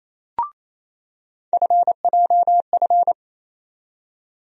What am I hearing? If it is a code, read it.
Morse code, heard as a keyed tone, about 27 wpm, 700 Hz. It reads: FJF
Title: USA Call Sign Postfixes - 3 Letters 28wpm